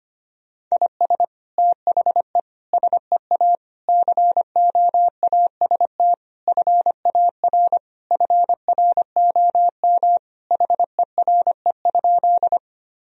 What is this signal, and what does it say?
Morse code, heard as a keyed tone, about 25 wpm, 700 Hz. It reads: IS THE SEA COAST FAR FROM HERE?
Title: Sentences from Top 1000 Words - Review Part 4 - 25wpm